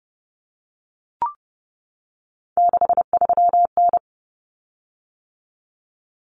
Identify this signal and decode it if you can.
Morse code, heard as a keyed tone, about 30 wpm, 700 Hz. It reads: 63N
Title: Common Cut Numbers - 3-Digit 30wpm